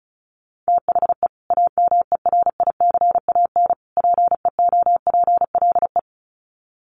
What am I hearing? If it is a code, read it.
Morse code, heard as a keyed tone, about 34 wpm, 700 Hz. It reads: THE AMERICAN PEOPLE
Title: Sets of 3 Words 35wpm